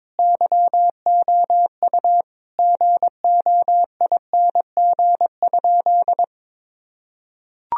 Morse code, heard as a keyed tone, about 22 wpm, 700 Hz. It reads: YOU GOING?
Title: Sentences from Top 1000 Words - Review Part 3 - 22wpm